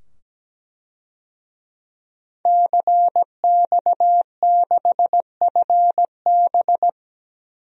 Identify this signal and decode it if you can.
Morse code, heard as a keyed tone, about 17 wpm, 700 Hz. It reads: CX6FB
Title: DXCC-100 Call Signs - 5 Characters - 17wpm